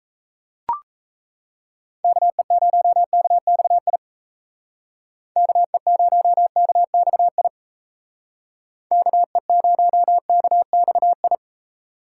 Morse code, heard as a keyed tone, about 33 wpm, 700 Hz. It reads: KE0KXI KE0KXI KE0KXI
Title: Speed Racing - USA 2x3 Call Signs 28wpm